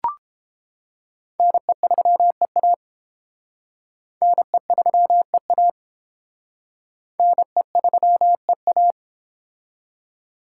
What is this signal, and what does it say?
Morse code, heard as a keyed tone, about 28 wpm, 700 Hz. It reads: NE3EA NE3EA NE3EA